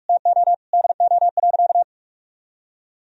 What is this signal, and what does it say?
Morse code, heard as a keyed tone, about 46 wpm, 700 Hz. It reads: TO DO.
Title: Sentences from Top 1000 Words - Review Part 3 - 45wpm